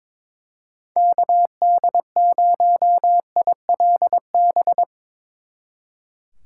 Morse code, heard as a keyed tone, about 24 wpm, 700 Hz. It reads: KD0ILB
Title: Speed Racing - USA 2x3 Call Signs 22wpm